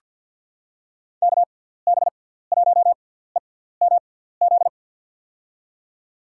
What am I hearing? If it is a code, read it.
Morse code, heard as a keyed tone, about 51 wpm, spaced as for 17 wpm, 700 Hz. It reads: KB1EMZ
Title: USA Call Signs - 6 Characters - 6x Character Spacing - 50wpm